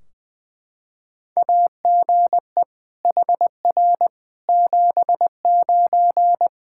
Morse code, heard as a keyed tone, about 20 wpm, 700 Hz. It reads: AGE HR 79
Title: Practice QSO - Exchange 3 - More Info - 20wpm